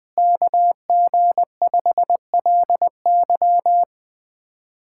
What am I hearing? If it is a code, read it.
Morse code, heard as a keyed tone, about 21 wpm, 700 Hz. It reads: KG5LY
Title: USA Callsign DE Callsign - 5 Characters - 20wpm